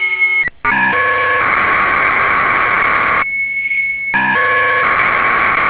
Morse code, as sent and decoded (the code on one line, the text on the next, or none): none